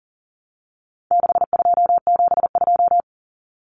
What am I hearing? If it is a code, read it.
Morse code, heard as a keyed tone, about 40 wpm, 700 Hz. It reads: 6272
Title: Common Cut Numbers - 4-Digit 40wpm